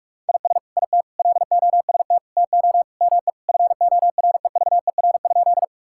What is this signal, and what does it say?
Morse code, heard as a keyed tone, about 41 wpm, 700 Hz. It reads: IS IT LOST TO ME FOREVER?
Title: Sentences from Top 1600 Words 45wpm